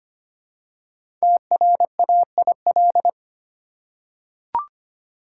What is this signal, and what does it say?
Morse code, heard as a keyed tone, about 25 wpm, 700 Hz. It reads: TRAIL
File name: 5 Letter Words - First 3 Letters Given - 25wpm